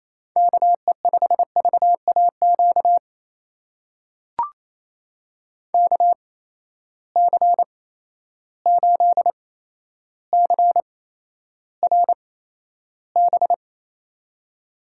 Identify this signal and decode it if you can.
Morse code, heard as a keyed tone, about 28 wpm, spaced as for 12 wpm, 700 Hz. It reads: KE5VAQ KC8CRB
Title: USA Call Signs - 8x Character Spacing - 28wpm